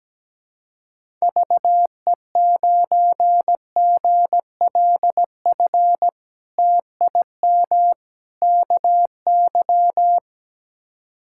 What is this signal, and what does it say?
Morse code, heard as a keyed tone, about 17 wpm, 700 Hz. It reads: VE9GLF TIM KY
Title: K1USN SST QSO Practice - 17wpm